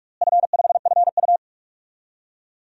Morse code, heard as a keyed tone, about 42 wpm, 700 Hz. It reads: F5FU
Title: DXCC-100 Call Signs - 4 Characters - Repeated 2x - 45wpm